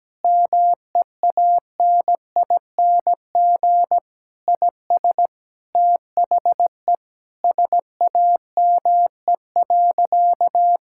Morse code, heard as a keyed tone, about 17 wpm, 700 Hz. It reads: MEANING IS THE SAME.
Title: Sentences from Top 1000 Words - Review Part 6 - 17wpm